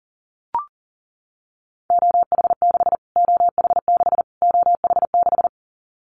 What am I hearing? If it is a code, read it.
Morse code, heard as a keyed tone, about 40 wpm, 700 Hz. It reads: OH6 OH6 OH6
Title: Callsign Prefixes for Top 25 DXCC - Repeated 3x - 40wpm